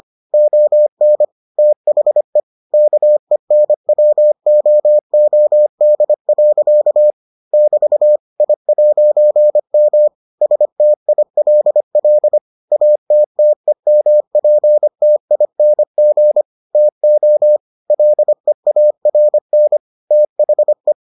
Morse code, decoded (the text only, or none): ON THE KENWOOD. <BT> I'M STILL ATTEMPTING TO LEARN THE